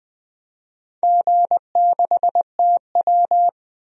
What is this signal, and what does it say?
Morse code, heard as a keyed tone, about 21 wpm, 700 Hz. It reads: G6TW
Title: DXCC-100 Call Signs - 4 Characters - 20wpm